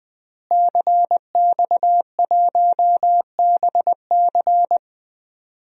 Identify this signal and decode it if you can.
Morse code, heard as a keyed tone, about 20 wpm, 700 Hz. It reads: CX1BC
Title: DXCC-100 Call Signs - 5 Characters - Repeated 2x - 20wpm